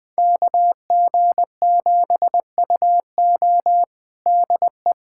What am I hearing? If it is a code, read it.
Morse code, heard as a keyed tone, about 21 wpm, 700 Hz. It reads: KG7UO DE
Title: USA Callsign DE Callsign - 5 Characters - 20wpm